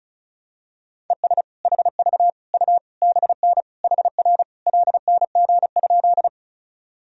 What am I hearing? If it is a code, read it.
Morse code, heard as a keyed tone, about 36 wpm, 700 Hz. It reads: ES HV U BN HR LNG?
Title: Sentences from Top 200 Words and Abbreviations 35wpm